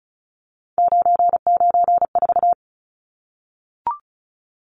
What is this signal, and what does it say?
Morse code, heard as a keyed tone, about 34 wpm, 700 Hz. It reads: 994
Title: Numbers - 3-Digit 35wpm